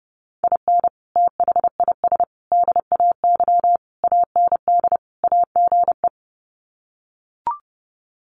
Morse code, heard as a keyed tone, about 30 wpm, 700 Hz. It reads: IN THIS DAY AND AGE
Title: Sets of 5 Words Encore 30wpm